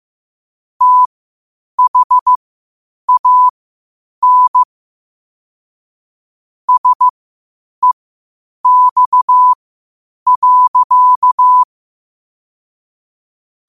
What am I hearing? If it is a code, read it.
Morse code, heard as a keyed tone, about 15 wpm, spaced as for 8 wpm, 1000 Hz. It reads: THAN SEX.
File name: qotdfiles / 8wpm